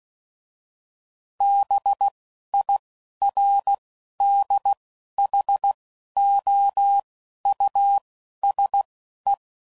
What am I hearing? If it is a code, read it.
Morse code, heard as a keyed tone, about 16 wpm, spaced as for 11 wpm, 800 Hz. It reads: BIRDHOUSE
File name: Suffixes / General / Houses